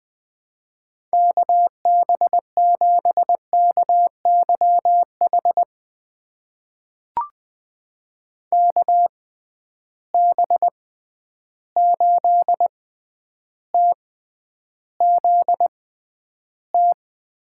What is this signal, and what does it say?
Morse code, heard as a keyed tone, about 20 wpm, spaced as for 10 wpm, 700 Hz. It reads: KB7KYH KB8TZT
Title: USA Call Signs - 6 Characters - 6x Character Spacing - 20wpm